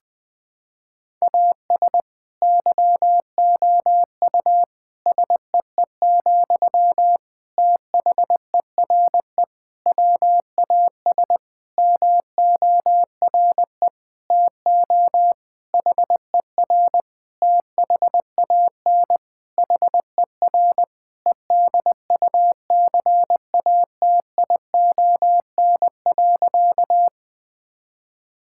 Morse code, heard as a keyed tone, about 20 wpm, 700 Hz. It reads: AS YOU SEE, THERE WAS MORE TO HER THAN HER EDUCATION.
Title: Sentences from Top 1000 Words - Review Part 6 - 20wpm